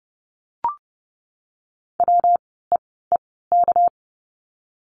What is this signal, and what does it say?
Morse code, heard as a keyed tone, about 30 wpm, spaced as for 17 wpm, 700 Hz. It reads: WEEK
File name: course / 30:10wpm / 237 Lesson 38f - Letter q - Words 30wpm